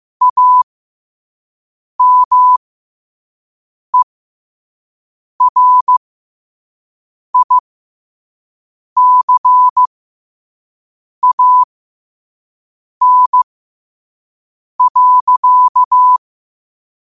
Morse code, heard as a keyed tone, about 15 wpm, spaced as for 5 wpm, 1000 Hz. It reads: AMERICAN.
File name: qotdfiles / 5wpm